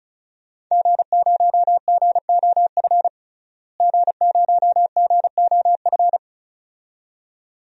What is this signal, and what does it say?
Morse code, heard as a keyed tone, about 36 wpm, 700 Hz. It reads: G0GOF G0GOF
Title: DXCC-100 Call Signs - 5 Characters - Repeated 2x - 35wpm